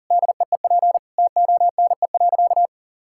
Morse code, heard as a keyed tone, about 40 wpm, 700 Hz. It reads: DEEP TONE.
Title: Sentences from Top 1600 Words 40wpm